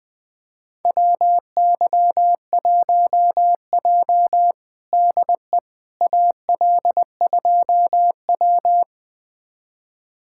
Morse code, heard as a keyed tone, about 20 wpm, 700 Hz. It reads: WY1J DE AL2W
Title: USA Callsign DE Callsign - 4 Characters - 20wpm